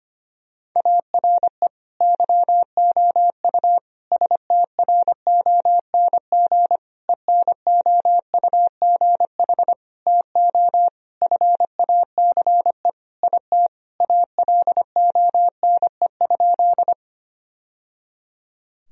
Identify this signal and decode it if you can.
Morse code, heard as a keyed tone, about 25 wpm, 700 Hz. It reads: ARE YOU STRONG ENOUGH TO FACE IT ALONE?
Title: Sentences from Top 1000 Words - Review Part 3 - 25wpm